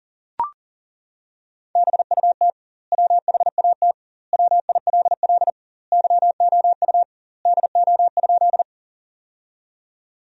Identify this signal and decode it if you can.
Morse code, heard as a keyed tone, about 37 wpm, 700 Hz. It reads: BUT WHAT WILL YOU DO?
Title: 2x Word-Spacing - Sentences from Top 100 Words - 40wpm